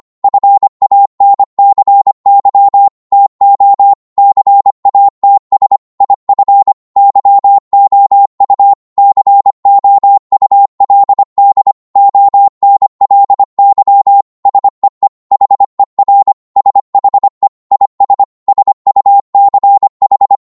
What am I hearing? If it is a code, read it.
Morse code, heard as a keyed tone, about 25 wpm, 800 Hz. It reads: FANCY TO CATS IF YOU COULD ONLY SEE HER SHE IS SUCH